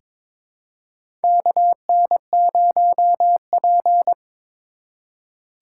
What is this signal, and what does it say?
Morse code, heard as a keyed tone, about 21 wpm, 700 Hz. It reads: KN0P
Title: USA Call Signs - 4 Characters - 2x Character Spacing - 22wpm